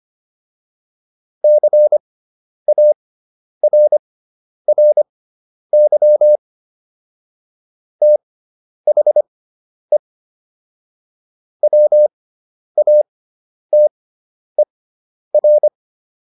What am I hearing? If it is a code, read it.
Morse code, heard as a keyed tone, about 25 wpm, spaced as for 10 wpm, 600 Hz. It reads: CARRY THE WATER